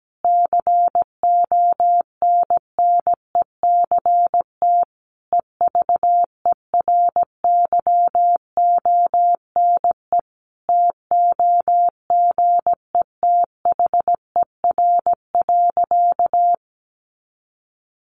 Morse code, heard as a keyed tone, about 17 wpm, 700 Hz. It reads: CONNECT EVERYONE TOGETHER.
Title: Sentences from Top 1500 Words 17wpm